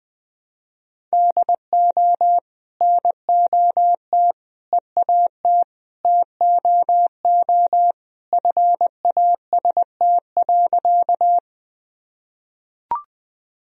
Morse code, heard as a keyed tone, about 21 wpm, 700 Hz. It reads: DO NOT EAT TOO FAST.
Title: Sentences from Top 1000 Words - Review Part 2 - 20wpm